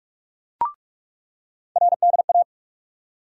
Morse code, heard as a keyed tone, about 42 wpm, 700 Hz. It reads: RDA
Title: USA Call Sign Postfixes - 3 Letters 45wpm